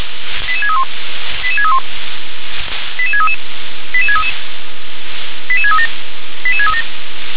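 Five-tone selective call (ZVEI), heard as quick runs of five tones, tones of about 70 ms. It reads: 88631 88631 88633 88633 88637 88637
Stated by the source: Received by SDR, ZVEI five-tone selective call.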